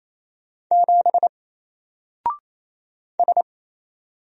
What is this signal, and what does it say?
Morse code, heard as a keyed tone, about 28 wpm, 700 Hz. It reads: 7 S
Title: Single Letter-Number Alt 28wpm